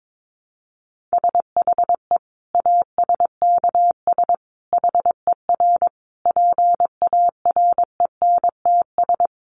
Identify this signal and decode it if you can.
Morse code, heard as a keyed tone, about 22 wpm, 700 Hz. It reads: SHE ASKS HER PARENTS
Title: Easy Phrases 22wpm